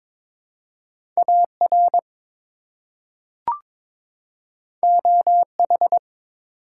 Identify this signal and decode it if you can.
Morse code, heard as a keyed tone, about 22 wpm, 700 Hz. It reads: AR OH
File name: ARRL Sections v2 - 22wpm